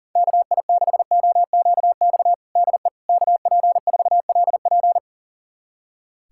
Morse code, heard as a keyed tone, about 40 wpm, 700 Hz. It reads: KI6OQX DE KP4LP
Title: USA Callsign DE Callsign 40wpm